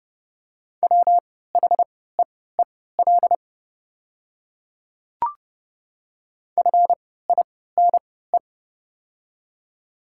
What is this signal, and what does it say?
Morse code, heard as a keyed tone, about 30 wpm, spaced as for 17 wpm, 700 Hz. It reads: WHEEL FINE
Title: 3x Character Spacing - Top 500 Words - 30wpm